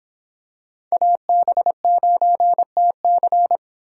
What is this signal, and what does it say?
Morse code, heard as a keyed tone, about 26 wpm, 700 Hz. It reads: AB9TC